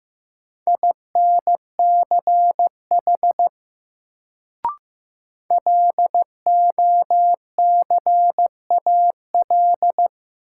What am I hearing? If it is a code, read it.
Morse code, heard as a keyed tone, about 15 wpm, 700 Hz. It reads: INCH LOCAL
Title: Sentences from Top 1000 Words - Review Part 3 - New Word Drill - 15wpm